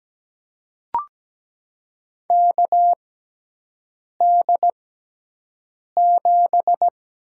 Morse code, heard as a keyed tone, about 17 wpm, spaced as for 6 wpm, 700 Hz. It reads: KD7
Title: USA Call Signs - 6 Characters - 6x Character Spacing - 17wpm